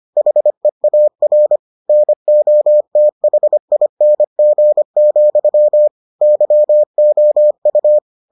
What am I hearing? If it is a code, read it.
Morse code, heard as a keyed tone, about 25 wpm, 600 Hz. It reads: HEAR NOTHING, YOU